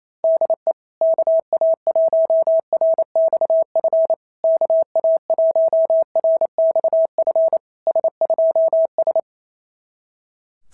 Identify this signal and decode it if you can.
Morse code, heard as a keyed tone, about 27 wpm, 660 Hz. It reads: DE KA1RXF KA1RXF S2S
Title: SOTA QSO Practice 28wpm